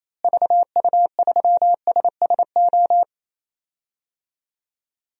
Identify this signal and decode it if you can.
Morse code, heard as a keyed tone, about 28 wpm, 700 Hz. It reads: VU3SSO
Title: DXCC-100 Call Signs - 6 Characters - Repeated 2x - 28wpm